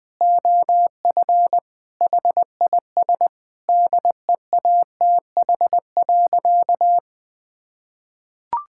Morse code, heard as a keyed tone, about 20 wpm, 700 Hz. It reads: OF HIS DEATH.
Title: Sentences with 6-Word Phrases - Spoken Beginning - 20wpm